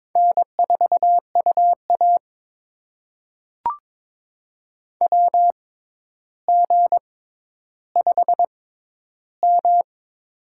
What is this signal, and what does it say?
Morse code, heard as a keyed tone, about 22 wpm, spaced as for 11 wpm, 700 Hz. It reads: N4UA WG5M
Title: USA Call Signs - 4 Characters - 6x Character Spacing - 22wpm